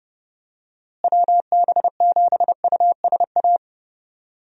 Morse code, heard as a keyed tone, about 30 wpm, 700 Hz. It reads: WB7USA
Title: USA Call Signs - 6 Characters - Repeated 3x - 30wpm